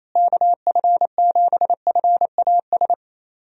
Morse code, heard as a keyed tone, about 27 wpm, 700 Hz. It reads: KF7FAS
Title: USA Call Signs - 6 Characters - Repeated 2x - 28wpm